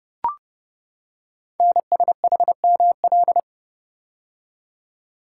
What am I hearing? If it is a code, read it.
Morse code, heard as a keyed tone, about 30 wpm, 700 Hz. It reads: NSHML
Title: Code Groups - 5 Letters - 30wpm